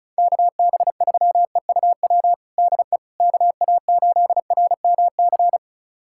Morse code, heard as a keyed tone, about 34 wpm, 700 Hz. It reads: KB3EUW DE KA8RMC
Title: USA Callsign DE Callsign - 6 Characters - 35wpm